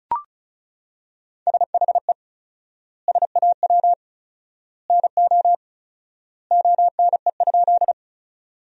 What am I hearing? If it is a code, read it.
Morse code, heard as a keyed tone, about 36 wpm, 700 Hz. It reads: SHE SAW NO ONE?
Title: 4x Word-Spacing - Sentences from Top 300 Words - 35wpm